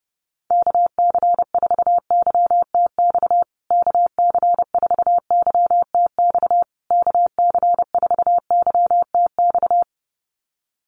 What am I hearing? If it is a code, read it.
Morse code, heard as a keyed tone, about 30 wpm, 700 Hz. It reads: KC4YTX KC4YTX KC4YTX
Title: USA Call Signs Repeated 3x 30wpm